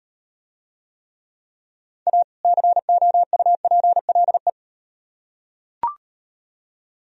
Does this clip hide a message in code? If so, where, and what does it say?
Yes: Morse code from 2.1 to 4.5 s, heard as a keyed tone, about 38 wpm, 700 Hz. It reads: A COUPLE